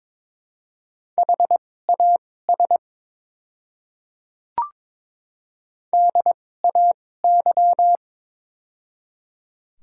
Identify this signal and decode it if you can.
Morse code, heard as a keyed tone, about 22 wpm, spaced as for 16 wpm, 700 Hz. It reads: HAS DAY
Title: 2x Character Spacing - Top 100 Words - 22wpm